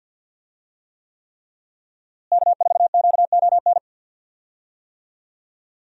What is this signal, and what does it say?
Morse code, heard as a keyed tone, about 51 wpm, 700 Hz. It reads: K4QON
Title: USA Call Signs - 5 Characters - Repeated 3x - 50wpm